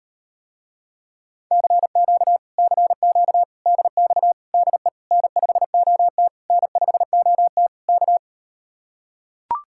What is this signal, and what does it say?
Morse code, heard as a keyed tone, about 39 wpm, 700 Hz. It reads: CQ CQ DX DE N5OT N5OT K